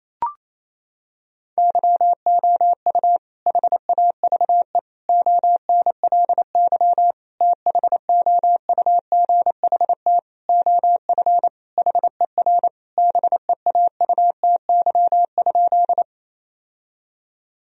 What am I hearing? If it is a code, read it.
Morse code, heard as a keyed tone, about 28 wpm, 700 Hz. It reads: YOU HAVE ONLY THOUGHT OF HER BEAUTY?
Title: Sentences from Top 1000 Words - Review Part 2 - 28wpm